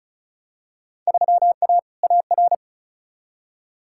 Morse code, heard as a keyed tone, about 34 wpm, 700 Hz. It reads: 3A AR
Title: Field Day Exchanges Encore - 35wpm